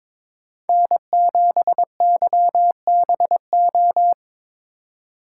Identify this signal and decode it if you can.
Morse code, heard as a keyed tone, about 22 wpm, 700 Hz. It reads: N7YBO
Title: USA Call Signs - 5 Characters - 2x Character Spacing - 22wpm